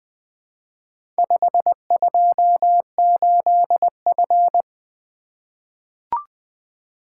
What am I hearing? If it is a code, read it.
Morse code, heard as a keyed tone, about 20 wpm, 700 Hz. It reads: 528F
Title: Four Letter-Number Combinations - 20wpm